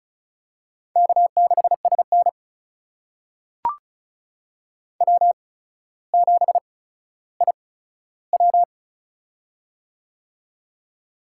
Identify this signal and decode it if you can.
Morse code, heard as a keyed tone, about 36 wpm, spaced as for 15 wpm, 700 Hz. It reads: K6SN W7IW
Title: USA Call Signs - 4 Characters - 8x Character Spacing - 35wpm